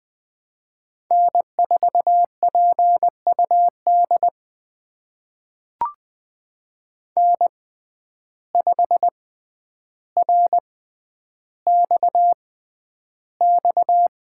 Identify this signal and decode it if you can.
Morse code, heard as a keyed tone, about 20 wpm, spaced as for 10 wpm, 700 Hz. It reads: N4PUD N5RXX
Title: USA Call Signs - 5 Characters - 6x Character Spacing - 20wpm